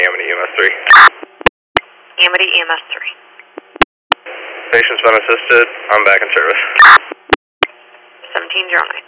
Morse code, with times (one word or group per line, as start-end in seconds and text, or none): none